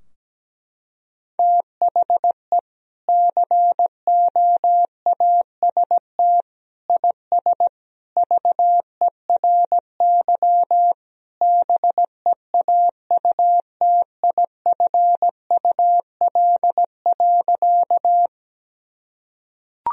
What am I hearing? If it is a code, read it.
Morse code, heard as a keyed tone, about 17 wpm, 700 Hz. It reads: THE COAST IS VERY BEAUTIFUL.
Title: Sentences from Top 1000 Words - Review Part 4 - 17wpm